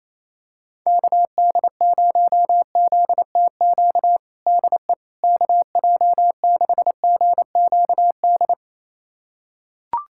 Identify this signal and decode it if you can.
Morse code, heard as a keyed tone, about 28 wpm, 700 Hz. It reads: KD0ZTQ DE KJ6GQD
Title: USA Callsign DE Callsign 28wpm